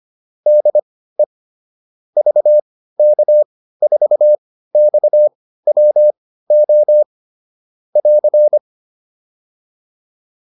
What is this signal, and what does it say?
Morse code, heard as a keyed tone, about 25 wpm, spaced as for 15 wpm, 600 Hz. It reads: DE VK4XWO <AR>
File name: QSO407 15